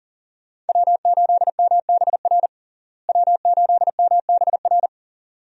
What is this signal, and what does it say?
Morse code, heard as a keyed tone, about 37 wpm, 700 Hz. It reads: W8MBR W8MBR
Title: USA Call Signs - 5 Characters - Repeated 2x - 40wpm